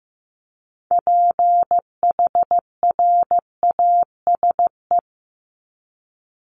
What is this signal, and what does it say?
Morse code, heard as a keyed tone, about 15 wpm, 700 Hz. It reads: PHRASE
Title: Top 1501-1750 Words 15wpm